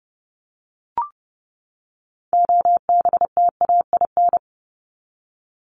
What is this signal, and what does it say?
Morse code, heard as a keyed tone, about 30 wpm, 700 Hz. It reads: OBTAIN
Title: ED Suffix - Top 100 Words with Base Words 30wpm